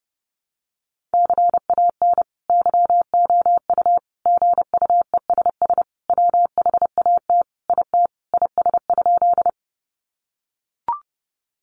Morse code, heard as a keyed tone, about 30 wpm, 700 Hz. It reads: CAN YOU GUESS WHAT IT IS?
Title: Sentences from Top 1500 Words 30wpm